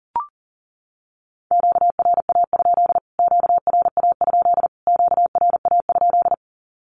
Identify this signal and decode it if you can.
Morse code, heard as a keyed tone, about 40 wpm, 700 Hz. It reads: QRA? QRA? QRA?
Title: Q-Codes Repeated 3x 40wpm